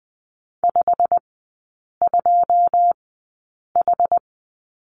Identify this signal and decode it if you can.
Morse code, heard as a keyed tone, about 20 wpm, spaced as for 8 wpm, 700 Hz. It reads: 52H
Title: 127 Lesson 21e - Number 6 - Three Characters 20wpm